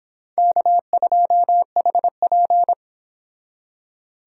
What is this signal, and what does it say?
Morse code, heard as a keyed tone, about 26 wpm, 700 Hz. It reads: K2HP